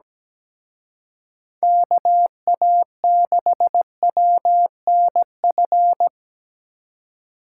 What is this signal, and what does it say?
Morse code, heard as a keyed tone, about 20 wpm, 700 Hz. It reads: KA6WNF
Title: Speed Racing - USA 2x3 Call Signs 17wpm